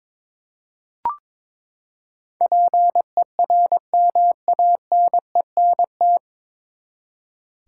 Morse code, heard as a keyed tone, about 22 wpm, 700 Hz. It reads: PERMANENT
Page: Morse Code Ninja practice 22wpm